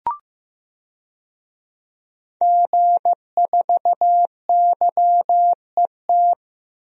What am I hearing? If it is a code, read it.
Morse code, heard as a keyed tone, about 15 wpm, 700 Hz. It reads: G4YET
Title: DXCC-100 Call Signs - 5 Characters - 15wpm